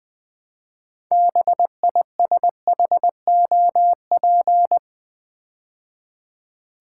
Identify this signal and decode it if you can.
Morse code, heard as a keyed tone, about 20 wpm, 700 Hz. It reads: BISHOP
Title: Build A Word - 2 Letters At A Time - 6-Letter Words - 20wpm